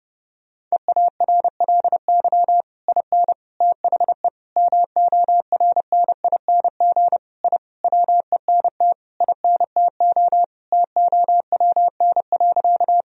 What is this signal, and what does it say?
Morse code, heard as a keyed tone, about 30 wpm, 700 Hz. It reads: EARLY IN THE MORNING I WENT INTO TOWN.
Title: Sentences from Top 1000 Words - Review Part 2 - 30wpm